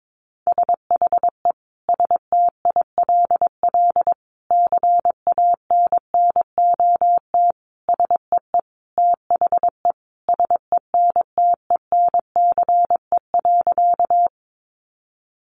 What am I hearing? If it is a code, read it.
Morse code, heard as a keyed tone, about 22 wpm, 700 Hz. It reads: SHE STILL CANNOT SEE THE SENTENCE.
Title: Additional Easy Phrases 22wpm